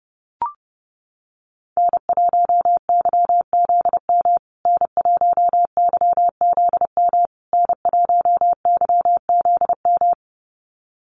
Morse code, heard as a keyed tone, about 30 wpm, 700 Hz. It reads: N1YZM N1YZM N1YZM
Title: USA Call Signs Repeated 3x 30wpm